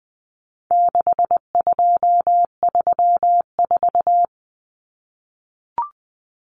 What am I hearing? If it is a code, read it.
Morse code, heard as a keyed tone, about 20 wpm, 700 Hz. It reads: 6234
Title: Common Cut Numbers - 4-Digit 20wpm